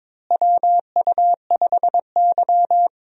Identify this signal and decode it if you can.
Morse code, heard as a keyed tone, about 22 wpm, 700 Hz. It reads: WU5Y
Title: USA Call Signs - 4 Characters - Repeated 3x - 22wpm